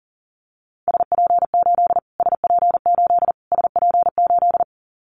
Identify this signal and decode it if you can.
Morse code, heard as a keyed tone, about 40 wpm, 700 Hz. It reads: SP8 SP8 SP8
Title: Callsign Prefixes for Top 25 DXCC - Repeated 3x - 40wpm